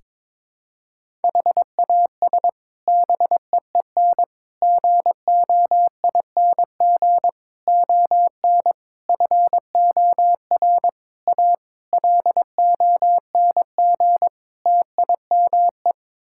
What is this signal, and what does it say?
Morse code, heard as a keyed tone, about 22 wpm, 700 Hz. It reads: HAS BEEN GOING ON FOR A LONG TIME
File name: practice / 22wpm / Sets of 8 Words - 22wpm